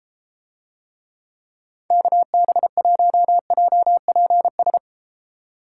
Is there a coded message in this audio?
Yes: Morse code from 1.9 to 4.8 s, heard as a keyed tone, about 30 wpm, 700 Hz. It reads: KB1JPS